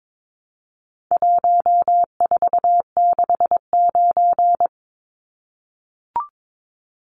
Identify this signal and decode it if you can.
Morse code, heard as a keyed tone, about 22 wpm, 700 Hz. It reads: 1469
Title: Numbers - 4-Digit 22wpm